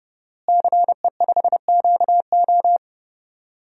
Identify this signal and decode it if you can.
Morse code, heard as a keyed tone, about 30 wpm, 700 Hz. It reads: CE5QO
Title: DXCC-100 Call Signs - 5 Characters - Repeated 2x - 30wpm